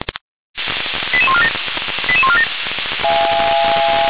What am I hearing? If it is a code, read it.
Five-tone selective call (ZVEI), heard as quick runs of five tones, tones of about 70 ms. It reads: 88147 88147